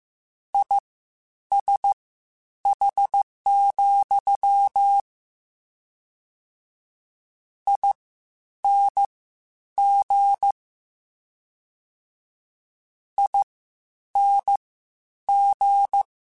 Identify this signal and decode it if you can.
Morse code, heard as a keyed tone, about 15 wpm, spaced as for 9 wpm, 800 Hz. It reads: ISH, ING ING
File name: General / General-mp3s / Suffixes sp @13